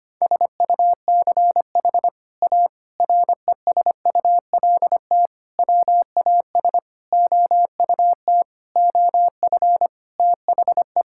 Morse code, heard as a keyed tone, about 25 wpm, 700 Hz. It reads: SUCH A RESULT WAS OUT OF THE
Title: Sentences from Top 1000 Words - Review Part 5 - 25wpm